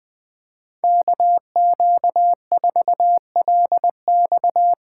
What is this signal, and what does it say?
Morse code, heard as a keyed tone, about 21 wpm, 700 Hz. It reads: KQ4LX
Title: USA Call Signs - 4x Character Spacing - 20wpm